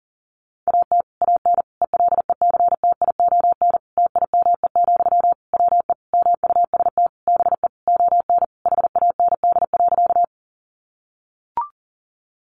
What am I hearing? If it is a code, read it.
Morse code, heard as a keyed tone, about 40 wpm, 700 Hz. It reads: AT AN ELECTION TIME, WE MUST BE ON HAND.
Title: Sentences from Top 1500 Words 40wpm